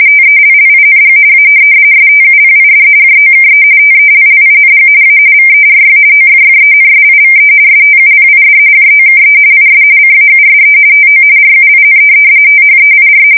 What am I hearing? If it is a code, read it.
RTTY, about 45.45 baud. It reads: TELLING A STUPID JOKE TELLING A VERY OFFENSIVE STUPID JOKE IS CAUSE TO RUIN THEIR
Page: 170 Shift 2125/2295